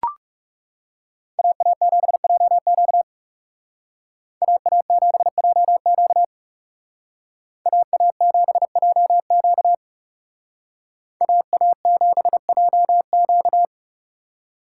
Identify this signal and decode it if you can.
Morse code, heard as a keyed tone, about 38 wpm, 700 Hz. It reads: AA7JQ AA7JQ AA7JQ AA7JQ